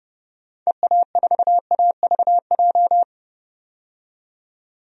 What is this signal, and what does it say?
Morse code, heard as a keyed tone, about 30 wpm, 700 Hz. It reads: EA4AVJ
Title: DXCC-100 Call Signs - 6 Characters - 30wpm